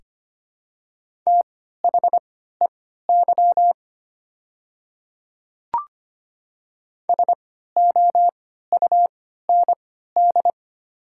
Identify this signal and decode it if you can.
Morse code, heard as a keyed tone, about 25 wpm, spaced as for 14 wpm, 690 Hz. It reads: THEY SOUND
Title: 3x Character Spacing - Top 100 Words - 25wpm